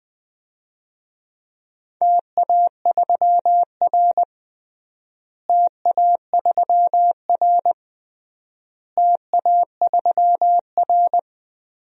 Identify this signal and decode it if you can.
Morse code, heard as a keyed tone, about 21 wpm, 700 Hz. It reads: TA3R TA3R TA3R
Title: DXCC-100 Call Signs - 4 Characters - Repeated 3x - 20wpm